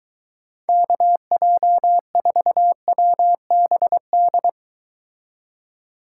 Morse code, heard as a keyed tone, about 23 wpm, 700 Hz. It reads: KJ4WBD